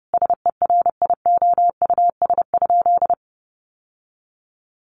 Morse code, heard as a keyed tone, about 30 wpm, 700 Hz. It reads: SERIOUS?
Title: Sentences from Top 1100 Words 30wpm